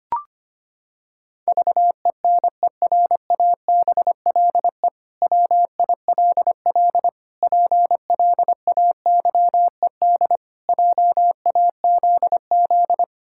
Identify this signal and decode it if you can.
Morse code, heard as a keyed tone, about 25 wpm, 700 Hz. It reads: VENERABLE WILL PLAYED JAZZ
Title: Pangram Challenge v2 - 25wpm